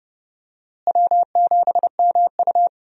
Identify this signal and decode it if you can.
Morse code, heard as a keyed tone, about 30 wpm, 700 Hz. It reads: W7MU
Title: USA Call Signs - 4 Characters - Repeated 2x - 30wpm